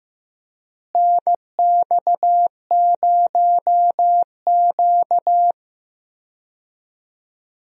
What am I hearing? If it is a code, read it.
Morse code, heard as a keyed tone, about 15 wpm, 700 Hz. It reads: NX0Q
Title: USA Call Signs - 4 Characters - Repeated 3x - 15wpm